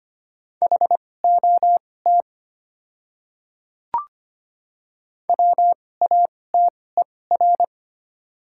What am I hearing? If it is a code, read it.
Morse code, heard as a keyed tone, about 25 wpm, spaced as for 18 wpm, 700 Hz. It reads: HOT WATER
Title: 2x Character Spacing - Top 100 Words - 25wpm